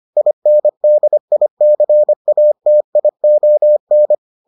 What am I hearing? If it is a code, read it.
Morse code, heard as a keyed tone, about 25 wpm, 600 Hz. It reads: INDICATION